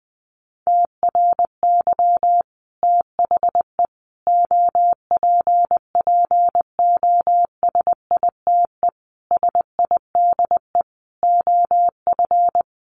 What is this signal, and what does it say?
Morse code, heard as a keyed tone, about 20 wpm, 700 Hz. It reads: TRY THE OPPOSITE SIDE OF
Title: Sentences from Top 1300 Words 20wpm